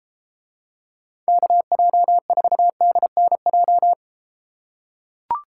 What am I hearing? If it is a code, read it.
Morse code, heard as a keyed tone, about 31 wpm, 700 Hz. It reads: KJ4DNJ